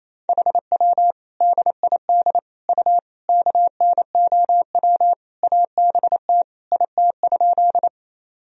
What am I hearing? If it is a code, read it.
Morse code, heard as a keyed tone, about 28 wpm, 700 Hz. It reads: HW DID U KNOW ABT IT?
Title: Sentences from Top 100 Words and Abbreviations 28wpm